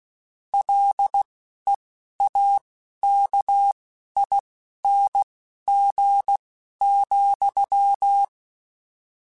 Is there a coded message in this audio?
Yes: Morse code from 0.5 to 8.2 s, heard as a keyed tone, about 16 wpm, spaced as for 11 wpm, 800 Hz. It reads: LEAKING,